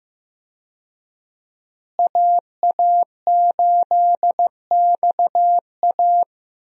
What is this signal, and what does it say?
Morse code, heard as a keyed tone, about 15 wpm, 700 Hz. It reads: AA8XA
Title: USA Call Signs - 5 Characters - Repeated 3x - 15wpm